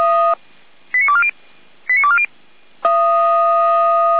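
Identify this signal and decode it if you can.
Five-tone selective call (ZVEI), heard as quick runs of five tones, tones of about 70 ms. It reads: 89240 89240
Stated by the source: ZVEI five-tone selective call received on SDR.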